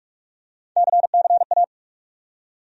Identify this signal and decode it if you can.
Morse code, heard as a keyed tone, about 46 wpm, 700 Hz. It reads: CCA
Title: USA Call Sign Postfixes - 3 Letters 45wpm